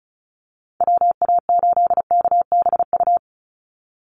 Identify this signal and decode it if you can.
Morse code, heard as a keyed tone, about 34 wpm, 700 Hz. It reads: WA8KBU
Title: Call Signs Repeated 3x - 35wpm